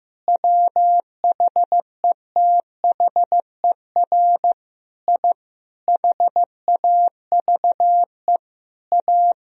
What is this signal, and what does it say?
Morse code, heard as a keyed tone, about 15 wpm, 700 Hz. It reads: WHETHER I HAVE A
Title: Sentences from Top 1000 Words - Review Part 3 - 15wpm